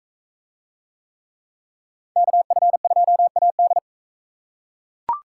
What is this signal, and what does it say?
Morse code, heard as a keyed tone, about 40 wpm, 700 Hz. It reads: KF2AD